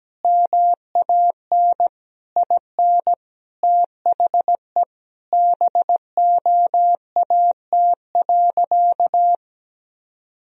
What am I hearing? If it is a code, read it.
Morse code, heard as a keyed tone, about 17 wpm, 700 Hz. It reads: MAN IN THE BOAT.
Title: Sentences from Top 1000 Words - Review Part 3 - 17wpm